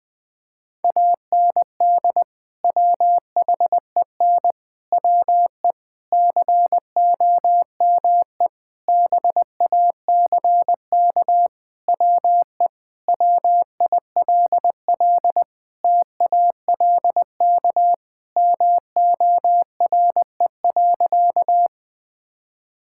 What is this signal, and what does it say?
Morse code, heard as a keyed tone, about 20 wpm, 700 Hz. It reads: AND WHEN WE COME BACK WE WILL TALK MORE.
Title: Sentences with 5-Word Phrases - 20wpm